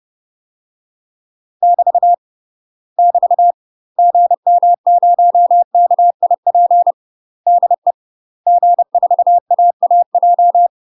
Morse code, heard as a keyed tone, about 30 wpm, 700 Hz. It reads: <BT> <BT> GM0KIP DE G4AAJ
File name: UK MASTER QSOs TRAINING / 30WPM / UK MASTER QSOs 0035